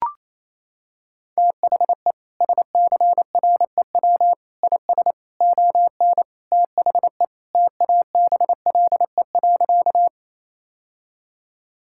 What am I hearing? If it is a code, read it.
Morse code, heard as a keyed tone, about 28 wpm, 700 Hz. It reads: THE SCREW IS ON THE TABLE.
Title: Sentences from Top 1600 Words 28wpm